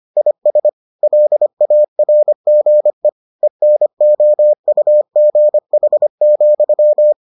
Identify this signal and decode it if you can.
Morse code, heard as a keyed tone, about 25 wpm, 600 Hz. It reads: IS LARGE ENOUGH,